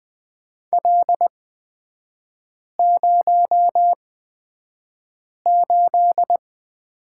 Morse code, heard as a keyed tone, about 20 wpm, spaced as for 5 wpm, 700 Hz. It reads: L08
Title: 243 Lesson 39e - Letter z - Three Characters 20wpm